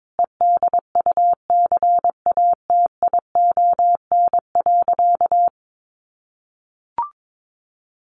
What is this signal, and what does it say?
Morse code, heard as a keyed tone, about 22 wpm, 700 Hz. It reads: EDUCATION.
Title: Sentences from Top 1500 Words 22wpm